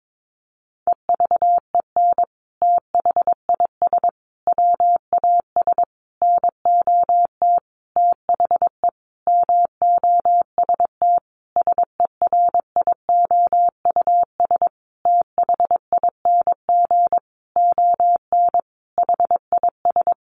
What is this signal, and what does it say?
Morse code, heard as a keyed tone, about 22 wpm, 700 Hz. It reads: EVEN THIS WAS NOT THE MOST SERIOUS THING ON HIS
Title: Sentences from Top 1100 Words 22wpm